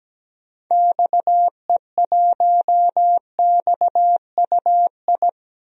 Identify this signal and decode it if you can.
Morse code, heard as a keyed tone, about 17 wpm, 700 Hz. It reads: XE1XUI
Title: DXCC-100 Call Signs - 6 Characters - 17wpm